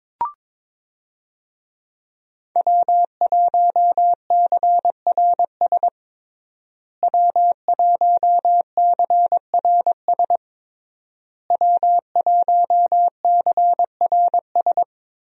Morse code, heard as a keyed tone, about 22 wpm, 700 Hz. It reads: W1CRS W1CRS W1CRS
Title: USA Call Signs - 5 Characters - Repeated 3x - 22wpm